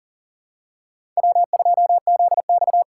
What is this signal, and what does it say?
Morse code, heard as a keyed tone, about 40 wpm, 700 Hz. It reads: W2ZX
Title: USA Call Signs - 4 Characters - 4x Character Spacing - 40wpm